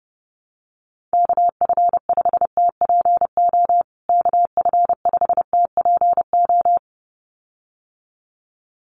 Morse code, heard as a keyed tone, about 30 wpm, 700 Hz. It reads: KF5TPO KF5TPO
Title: USA Call Signs Repeated 2x 30wpm